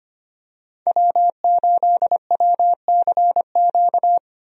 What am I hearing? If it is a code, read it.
Morse code, heard as a keyed tone, about 25 wpm, 700 Hz. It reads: W8WCQ
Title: USA Call Signs - 5 Characters - 25wpm